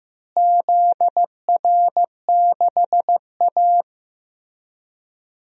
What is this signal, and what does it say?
Morse code, heard as a keyed tone, about 15 wpm, 700 Hz. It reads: ZR6A
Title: DXCC-100 Call Signs - 4 Characters - Repeated 2x - 15wpm